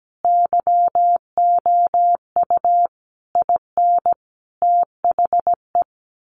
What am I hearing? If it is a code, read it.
Morse code, heard as a keyed tone, about 17 wpm, 700 Hz. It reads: YOU IN THE
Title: Sentences from Top 900 Words 17wpm